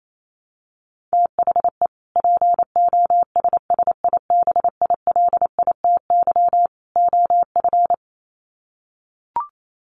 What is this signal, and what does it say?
Morse code, heard as a keyed tone, about 28 wpm, 700 Hz. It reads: THE POSSIBILITY OF
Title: Sets of 3 Words 28wpm